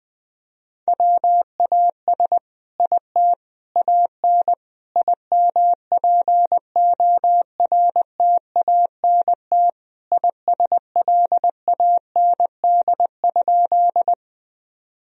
Morse code, heard as a keyed tone, about 21 wpm, 700 Hz. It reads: WAS IT AN IMPORTANT ISLAND?
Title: Sentences from Top 1000 Words - Review Part 3 - 20wpm